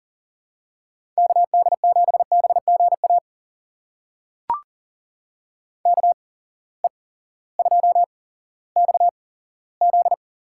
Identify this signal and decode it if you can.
Morse code, heard as a keyed tone, about 40 wpm, spaced as for 18 wpm, 700 Hz. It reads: KD7BGA KE2XZ
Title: USA Call Signs - 8x Character Spacing - 40wpm